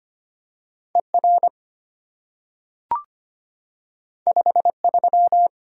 Morse code, heard as a keyed tone, about 25 wpm, 700 Hz. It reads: ER 53
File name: Two Letter-Number Combinations - 25wpm